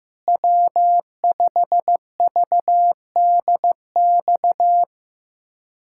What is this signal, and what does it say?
Morse code, heard as a keyed tone, about 15 wpm, 700 Hz. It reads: W5VDX
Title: USA Callsign DE Callsign 15wpm